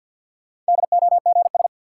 Morse code, heard as a keyed tone, about 47 wpm, 700 Hz. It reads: DOGS
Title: Sentences from Top 1600 Words - New Word Drill Alt - 50wpm